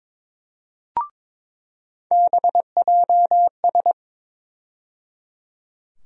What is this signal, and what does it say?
Morse code, heard as a keyed tone, about 22 wpm, 700 Hz. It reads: BJS
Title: USA Call Sign Postfixes - 3 Letters 22wpm